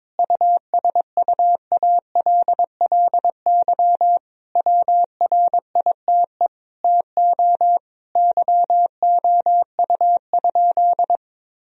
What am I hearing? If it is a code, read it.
Morse code, heard as a keyed tone, about 22 wpm, 700 Hz. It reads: USUALLY WRITE TO YOU?
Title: Sentences from Top 1000 Words - Review Part 3 - 22wpm